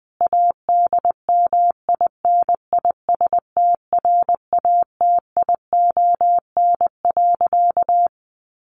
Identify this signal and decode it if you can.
Morse code, heard as a keyed tone, about 20 wpm, 700 Hz. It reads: ADMINISTRATION.
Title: Sentences from Top 1300 Words 20wpm